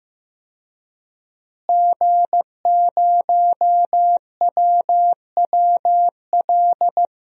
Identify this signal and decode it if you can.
Morse code, heard as a keyed tone, about 15 wpm, 700 Hz. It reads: G0WWL
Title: DXCC-100 Call Signs - 5 Characters - Repeated 3x - 15wpm